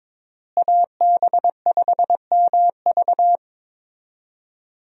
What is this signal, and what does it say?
Morse code, heard as a keyed tone, about 22 wpm, 700 Hz. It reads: AB5MV
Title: USA Call Signs - 5 Characters - Repeated 3x - 22wpm